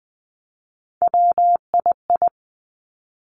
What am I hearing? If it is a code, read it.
Morse code, heard as a keyed tone, about 20 wpm, 700 Hz. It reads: WII
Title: Three Challenging Letter-Number Combinations 20wpm